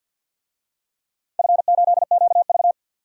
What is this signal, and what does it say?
Morse code, heard as a keyed tone, about 47 wpm, 700 Hz. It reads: F8QV
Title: DXCC-100 Call Signs - 4 Characters - 50wpm